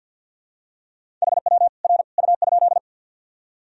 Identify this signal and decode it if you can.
Morse code, heard as a keyed tone, about 51 wpm, 700 Hz. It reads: HW R U?
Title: Sentences from Top 100 Words and Abbreviations 50wpm